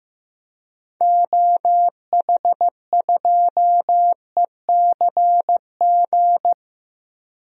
Morse code, heard as a keyed tone, about 15 wpm, 700 Hz. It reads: OH2ECG
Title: DXCC-100 Call Signs - 6 Characters - Repeated 3x - 15wpm